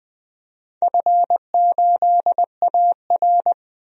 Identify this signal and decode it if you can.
Morse code, heard as a keyed tone, about 20 wpm, 700 Hz. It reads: F8AR
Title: DXCC-100 Call Signs - 4 Characters - 20wpm